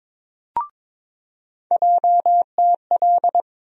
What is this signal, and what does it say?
Morse code, heard as a keyed tone, about 23 wpm, 700 Hz. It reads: JTL
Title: USA Call Sign Postfixes - 3 Letters 22wpm